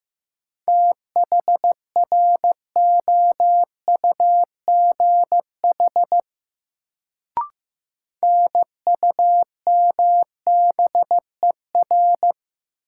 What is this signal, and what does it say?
Morse code, heard as a keyed tone, about 15 wpm, 700 Hz. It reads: THROUGH NUMBER
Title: Sets of 2 Words - New Word Drill - 15wpm